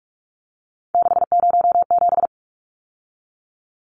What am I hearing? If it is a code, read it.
Morse code, heard as a keyed tone, about 44 wpm, 700 Hz. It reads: 607
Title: Numbers - 3-Digit 45wpm